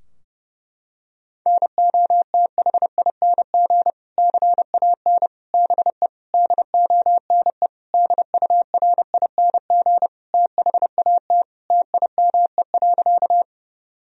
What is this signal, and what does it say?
Morse code, heard as a keyed tone, about 32 wpm, 700 Hz. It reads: NOTHING CAN BE DONE DURING THAT TIME.
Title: Sentences from Top 1000 Words - Review Part 2 - 30wpm